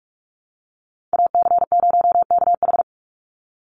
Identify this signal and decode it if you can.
Morse code, heard as a keyed tone, about 44 wpm, 700 Hz. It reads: AC0KH
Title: USA Call Signs 45wpm